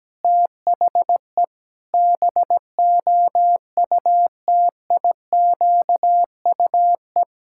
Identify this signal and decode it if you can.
Morse code, heard as a keyed tone, about 17 wpm, 700 Hz. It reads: THE BOUTIQUE
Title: Pangram Challenge v2 - 17wpm